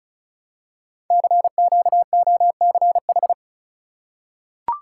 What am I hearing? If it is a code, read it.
Morse code, heard as a keyed tone, about 36 wpm, 690 Hz. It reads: CQOCH
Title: Code Groups - 5 Letter:Numbers - 35wpm